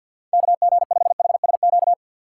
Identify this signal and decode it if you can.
Morse code, heard as a keyed tone, about 52 wpm, 700 Hz. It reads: KG5HSQ
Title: USA Call Signs - 6 Characters - 6x Character Spacing - 50wpm